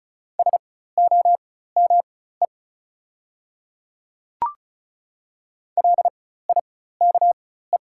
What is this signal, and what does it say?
Morse code, heard as a keyed tone, about 36 wpm, spaced as for 16 wpm, 700 Hz. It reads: SOME LIKE
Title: 4x Character Spacing - Top 300 Words - 35wpm